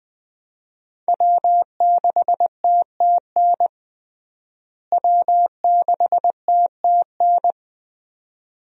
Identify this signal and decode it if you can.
Morse code, heard as a keyed tone, about 20 wpm, 700 Hz. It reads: W6TTN W6TTN
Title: USA Call Signs - 5 Characters - Repeated 2x - 20wpm